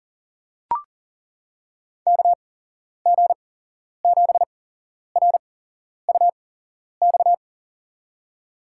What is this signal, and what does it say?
Morse code, heard as a keyed tone, about 37 wpm, spaced as for 11 wpm, 700 Hz. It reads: KG7RUX
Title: USA Call Signs - 6 Characters - 8x Character Spacing - 40wpm